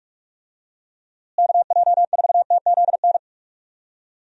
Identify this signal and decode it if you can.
Morse code, heard as a keyed tone, about 46 wpm, 700 Hz. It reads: KJ4TZN
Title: USA Call Signs - 6 Characters - Repeated 2x - 45wpm